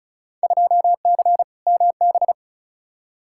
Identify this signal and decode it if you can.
Morse code, heard as a keyed tone, about 36 wpm, 700 Hz. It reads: 2C MB
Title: Field Day Exchanges Encore - 35wpm